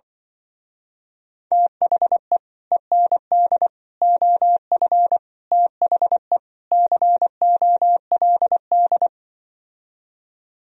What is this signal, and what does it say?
Morse code, heard as a keyed tone, about 24 wpm, 700 Hz. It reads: THE END OF THE COLD
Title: Speed-Racing - Sets of 5 Words 20wpm